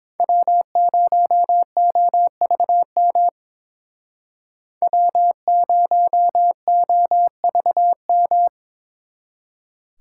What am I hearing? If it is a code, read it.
Morse code, heard as a keyed tone, about 26 wpm, 700 Hz. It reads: W0OVM W0OVM
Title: Speed Racing - USA 1x3 Call Signs 22wpm